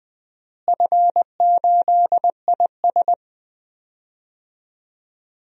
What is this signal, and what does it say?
Morse code, heard as a keyed tone, about 20 wpm, 700 Hz. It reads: F8IS
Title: DXCC-100 Call Signs - 4 Characters - Repeated 3x - 20wpm